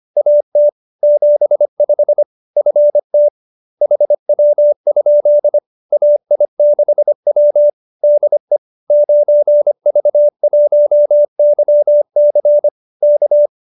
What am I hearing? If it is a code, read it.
Morse code, heard as a keyed tone, about 25 wpm, 590 Hz. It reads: AT 75 FT HW? AI6W DE 9V1YC K
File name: qso207b 25